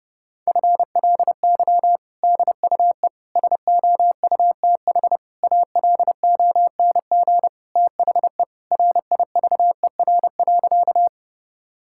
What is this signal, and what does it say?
Morse code, heard as a keyed tone, about 30 wpm, 700 Hz. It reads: FLY DUE SOUTH ALONG THE RIVER.
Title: Sentences from Top 1000 Words - Review Part 6 - 30wpm